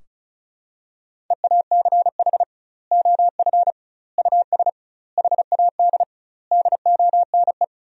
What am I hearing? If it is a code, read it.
Morse code, heard as a keyed tone, about 36 wpm, 700 Hz. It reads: EACH OF US HAD DONE
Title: 2x Word-Spacing - Sentences from Top 500 Words - 35wpm